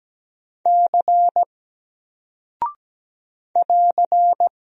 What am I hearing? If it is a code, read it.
Morse code, heard as a keyed tone, about 17 wpm, 700 Hz. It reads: C <AR>
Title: Single Character (Letter:Number:Punctuation:Prosign) Alt 17wpm